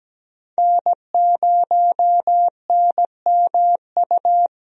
Morse code, heard as a keyed tone, about 17 wpm, 700 Hz. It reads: N0NMU
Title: USA Call Signs - 5 Characters - 17wpm